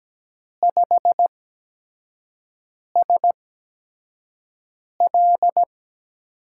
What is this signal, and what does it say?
Morse code, heard as a keyed tone, about 17 wpm, spaced as for 5 wpm, 700 Hz. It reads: 5SL
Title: USA Call Signs - 5 Characters - 8x Character Spacing - 17wpm